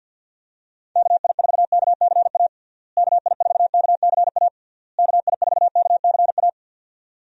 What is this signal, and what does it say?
Morse code, heard as a keyed tone, about 52 wpm, 700 Hz. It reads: KI4KCA KI4KCA KI4KCA
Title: USA Call Signs - 6 Characters - Repeated 3x - 50wpm